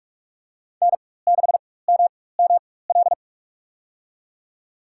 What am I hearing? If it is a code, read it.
Morse code, heard as a keyed tone, about 44 wpm, spaced as for 21 wpm, 700 Hz. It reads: N6MML
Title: USA Call Signs - 5 Characters - 4x Character Spacing - 45wpm